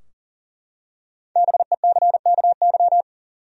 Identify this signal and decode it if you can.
Morse code, heard as a keyed tone, about 40 wpm, 700 Hz. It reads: BECKY
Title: USA Common Names - Max 5 Letters - 40wpm